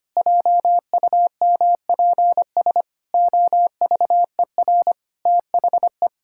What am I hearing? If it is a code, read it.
Morse code, heard as a keyed tone, about 25 wpm, 700 Hz. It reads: JUMPS OVER THE
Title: Pangram Challenge v2 - 25wpm